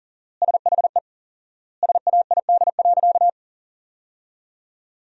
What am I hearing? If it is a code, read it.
Morse code, heard as a keyed tone, about 40 wpm, 700 Hz. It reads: SHE SAID.
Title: 4x Word-Spacing - Sentences from Top 400 Words - 40wpm